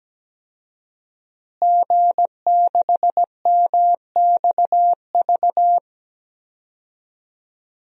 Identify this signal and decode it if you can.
Morse code, heard as a keyed tone, about 17 wpm, 700 Hz. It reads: G6MXV
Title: DXCC-100 Call Signs - 5 Characters - Repeated 3x - 17wpm